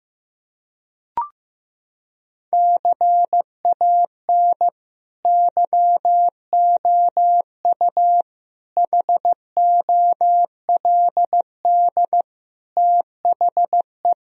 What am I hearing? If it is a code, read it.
Morse code, heard as a keyed tone, about 15 wpm, 700 Hz. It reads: CAN YOU HOLD THE
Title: Sentences from Top 1000 Words - Review Part 2 - 15wpm